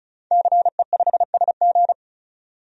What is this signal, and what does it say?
Morse code, heard as a keyed tone, about 36 wpm, 700 Hz. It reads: CE5SG
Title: DXCC-100 Call Signs - 5 Characters - 35wpm